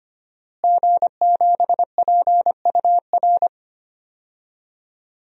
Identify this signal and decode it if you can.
Morse code, heard as a keyed tone, about 25 wpm, 700 Hz. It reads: G7PUR
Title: DXCC-100 Call Signs - 5 Characters - Repeated 3x - 25wpm